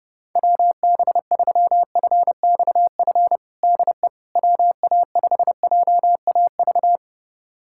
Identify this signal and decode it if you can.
Morse code, heard as a keyed tone, about 30 wpm, 700 Hz. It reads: WB3FXF DE WA5JAV
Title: USA Callsign DE Callsign 30wpm